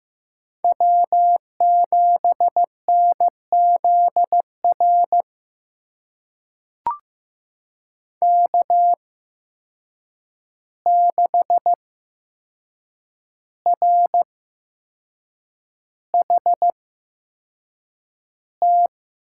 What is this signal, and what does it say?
Morse code, heard as a keyed tone, about 15 wpm, spaced as for 6 wpm, 700 Hz. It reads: W7NZR K6RHT
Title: USA Call Signs - 8x Character Spacing - 15wpm